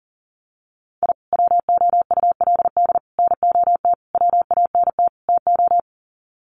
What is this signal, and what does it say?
Morse code, heard as a keyed tone, about 40 wpm, 700 Hz. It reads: I WOULD NOT WANT TO
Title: Sets of 5 Words 40wpm